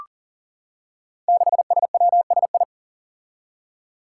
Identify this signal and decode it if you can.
Morse code, heard as a keyed tone, about 40 wpm, 700 Hz. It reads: 6SWSI
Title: Five Challenging Letter-Number Combinations 40wpm